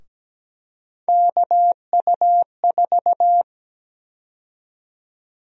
Morse code, heard as a keyed tone, about 17 wpm, 700 Hz. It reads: KU4
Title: USA Call Sign Prefixes - Repeated 2x - 17wpm